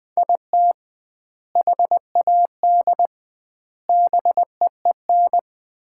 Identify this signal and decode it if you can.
Morse code, heard as a keyed tone, about 20 wpm, 700 Hz. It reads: IT HAD BEEN
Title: 2x Word-Spacing - Sets of 3 Words - 20wpm